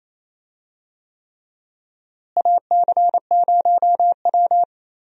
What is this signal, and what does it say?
Morse code, heard as a keyed tone, about 28 wpm, 700 Hz. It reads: AC0W
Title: USA Call Signs - 4 Characters - Repeated 3x - 28wpm